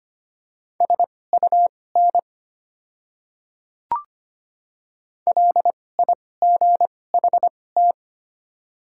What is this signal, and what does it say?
Morse code, heard as a keyed tone, about 25 wpm, spaced as for 18 wpm, 700 Hz. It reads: SUN LIGHT
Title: 2x Character Spacing - Top 500 Words - 25wpm